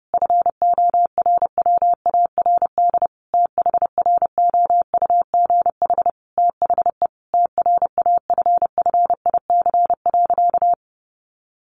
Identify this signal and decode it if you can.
Morse code, heard as a keyed tone, about 30 wpm, 700 Hz. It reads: FORWARD THROUGH THE TRAFFIC.
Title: Sentences from Top 1300 Words 30wpm